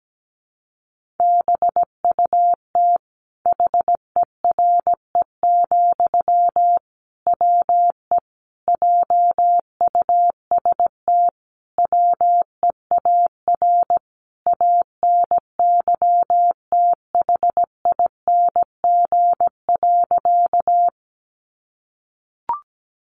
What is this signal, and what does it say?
Morse code, heard as a keyed tone, about 17 wpm, 700 Hz. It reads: BUT HERE, WE JUST WEAR ANYTHING.
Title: Sentences from Top 1500 Words 17wpm